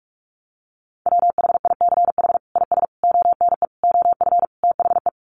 Morse code, heard as a keyed tone, about 44 wpm, 700 Hz. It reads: WHICH IS ONE OF THE
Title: Sets of 5 Words Encore 45wpm